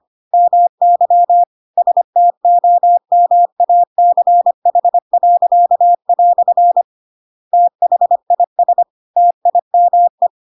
Morse code, heard as a keyed tone, about 25 wpm, 700 Hz. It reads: MY STOMACH." THIS TIME
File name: Pinocchio / Pinocchio ch1 25 25 ik1hge